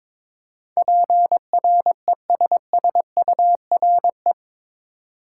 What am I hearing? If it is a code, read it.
Morse code, heard as a keyed tone, about 22 wpm, 700 Hz. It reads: PRESSURE
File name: practice / 22wpm / Eight Letter Words - 22wpm